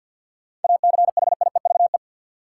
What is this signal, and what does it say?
Morse code, heard as a keyed tone, about 49 wpm, 700 Hz. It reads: ACHIEVE
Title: Sentences from Top 1000 Words - Review Part 5 - New Word Drill - 50wpm